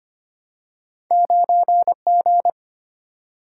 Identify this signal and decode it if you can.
Morse code, heard as a keyed tone, about 25 wpm, 700 Hz. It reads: 9G
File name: Two Letter-Number Combinations - 25wpm